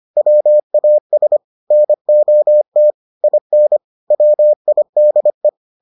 Morse code, heard as a keyed tone, about 25 wpm, 600 Hz. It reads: WAS NOT IN WIDE